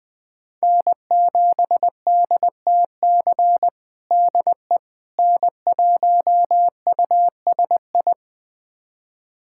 Morse code, heard as a keyed tone, about 20 wpm, 700 Hz. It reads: N7DTC DE N1USI
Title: USA Callsign DE Callsign - 5 Characters - 20wpm